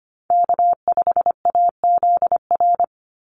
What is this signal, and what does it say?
Morse code, heard as a keyed tone, about 25 wpm, 700 Hz. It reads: K5AZR
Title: Call Signs Repeated 2x - 25wpm